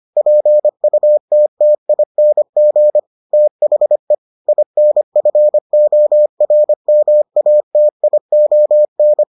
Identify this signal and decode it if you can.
Morse code, heard as a keyed tone, about 25 wpm, 600 Hz. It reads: PUTTING THE INFORMATION